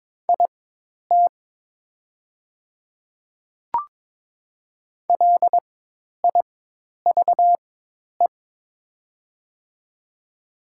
Morse code, heard as a keyed tone, about 22 wpm, spaced as for 10 wpm, 700 Hz. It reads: IT LIVE
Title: 4x Character Spacing - Top 200 Words - 22wpm